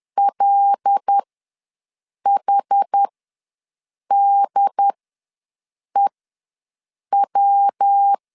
Morse code, heard as a keyed tone, about 10 wpm, spaced as for 6 wpm, 800 Hz. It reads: LHDEW
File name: LEZIONI / 5 LBVC / 5 L ws 10wpm 3636